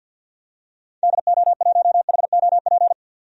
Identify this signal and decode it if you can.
Morse code, heard as a keyed tone, about 53 wpm, 700 Hz. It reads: DO1HOP
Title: DXCC-100 Call Signs - 6 Characters - 50wpm